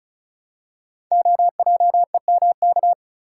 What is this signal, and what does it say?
Morse code, heard as a keyed tone, about 36 wpm, 700 Hz. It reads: OJEMK
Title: Code Groups - 5 Letters - 35wpm